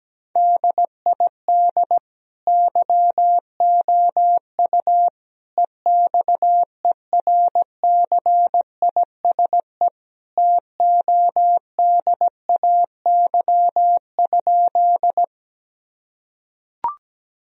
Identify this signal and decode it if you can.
Morse code, heard as a keyed tone, about 17 wpm, 700 Hz. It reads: DID YOU EXERCISE TODAY?
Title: Sentences from Top 1000 Words - Review Part 4 - 17wpm